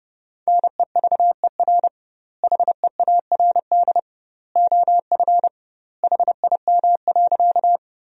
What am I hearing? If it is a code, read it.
Morse code, heard as a keyed tone, about 30 wpm, 700 Hz. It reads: NEVER HEARD OF HIM.
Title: 2x Word-Spacing - Sentences from Top 400 Words - 30wpm